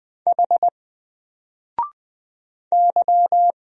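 Morse code, heard as a keyed tone, about 20 wpm, 700 Hz. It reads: H Y
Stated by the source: Single letter and number Morse drill at 20 wpm, from N4LCD.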